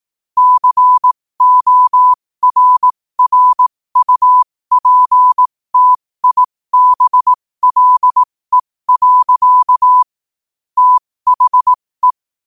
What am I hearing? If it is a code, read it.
Morse code, heard as a keyed tone, about 18 wpm, spaced as for 15 wpm, 1000 Hz. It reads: CORRUPTIBLE. THE
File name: qotdfiles / 15wpm